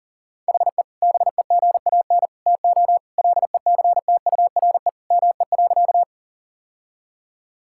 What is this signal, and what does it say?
Morse code, heard as a keyed tone, about 40 wpm, 700 Hz. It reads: HE BEGAN TO LECTURE ME.
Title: Sentences from Top 1600 Words 40wpm